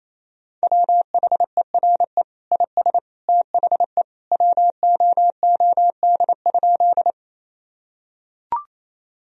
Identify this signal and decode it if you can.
Morse code, heard as a keyed tone, about 28 wpm, 700 Hz. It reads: WHERE IS THE WOOD?
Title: Sentences from Top 1000 Words - Review Part 2 - 28wpm